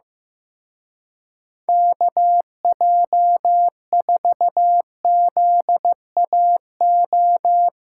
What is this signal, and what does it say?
Morse code, heard as a keyed tone, about 17 wpm, 700 Hz. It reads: KJ4ZAO
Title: Speed Racing - USA 2x3 Call Signs 15wpm